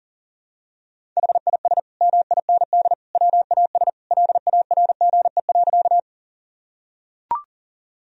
Morse code, heard as a keyed tone, about 40 wpm, 700 Hz. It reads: HIS MIND WAS LARGE.
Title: Sentences from Top 1000 Words - Review Part 1 - 40wpm